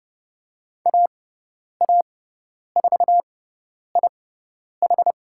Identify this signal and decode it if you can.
Morse code, heard as a keyed tone, about 30 wpm, spaced as for 10 wpm, 700 Hz. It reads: AA4IH
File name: Lesson 18g - Letter m - Call Signs 30wpm